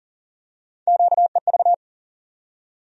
Morse code, heard as a keyed tone, about 40 wpm, 700 Hz. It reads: QEV
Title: USA Call Sign Postfixes - 3 Letters 40wpm